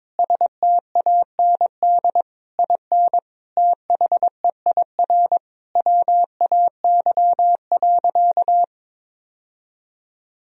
Morse code, heard as a keyed tone, about 22 wpm, 700 Hz. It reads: STAND IN THEIR WAY.
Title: Sentences from Top 1000 Words - Review Part 1 - 22wpm